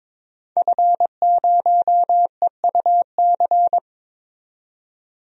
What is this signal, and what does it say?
Morse code, heard as a keyed tone, about 22 wpm, 700 Hz. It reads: F0EUC
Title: DXCC-100 Call Signs - 5 Characters - 22wpm